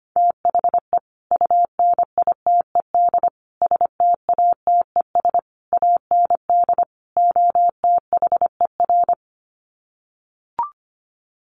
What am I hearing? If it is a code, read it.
Morse code, heard as a keyed tone, about 25 wpm, 700 Hz. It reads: THE UNITED STATES AND OTHER
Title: Sets of 5 Words Encore 25wpm